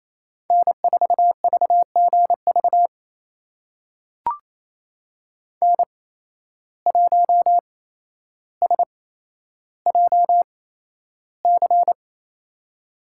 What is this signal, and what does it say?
Morse code, heard as a keyed tone, about 28 wpm, spaced as for 12 wpm, 700 Hz. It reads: N4VGV N1SJC
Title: USA Call Signs - 5 Characters - 8x Character Spacing - 28wpm